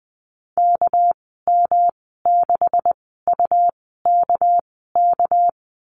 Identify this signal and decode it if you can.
Morse code, heard as a keyed tone, about 20 wpm, spaced as for 14 wpm, 700 Hz. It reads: KM6UKK
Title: 200 Lesson 32g - Letter k - Call Signs 20wpm